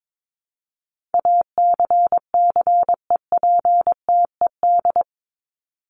Morse code, heard as a keyed tone, about 22 wpm, 700 Hz. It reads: ACCEPTED
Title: ED Suffix - Top 100 Words with Base Words 22wpm